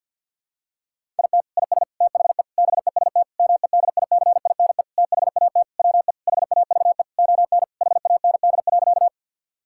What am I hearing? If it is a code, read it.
Morse code, heard as a keyed tone, about 51 wpm, 700 Hz. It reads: IT IS THE BEST MEDICINE THAT WE HAVE ON HAND.
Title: Sentences from Top 1000 Words - Review Part 6 - 50wpm